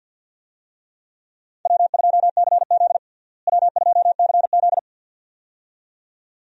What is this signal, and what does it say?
Morse code, heard as a keyed tone, about 52 wpm, 700 Hz. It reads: W2CZ W2CZ
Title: USA Call Signs - 4 Characters - Repeated 2x - 50wpm